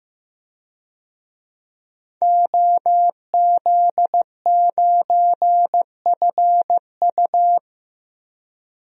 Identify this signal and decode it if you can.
Morse code, heard as a keyed tone, about 15 wpm, 700 Hz. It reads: OZ9FU
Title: DXCC-100 Call Signs - 5 Characters - 15wpm